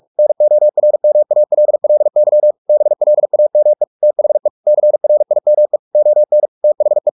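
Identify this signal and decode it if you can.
Morse code, heard as a keyed tone, about 44 wpm, 600 Hz. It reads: NORMALLY BLAME THE CRIME ON THE
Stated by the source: Morse practice at 45 wpm, the novel Never by Ken Follett in English, from CW med Gnister.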